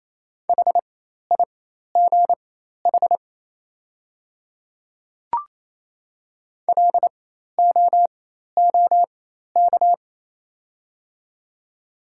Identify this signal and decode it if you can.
Morse code, heard as a keyed tone, about 27 wpm, spaced as for 13 wpm, 700 Hz. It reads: HIGH LOOK
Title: 4x Character Spacing - Top 500 Words - 28wpm